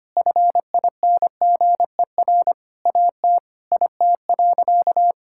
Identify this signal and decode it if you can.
Morse code, heard as a keyed tone, about 25 wpm, 700 Hz. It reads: FINGER AT IT.
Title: Sentences from Top 1600 Words 25wpm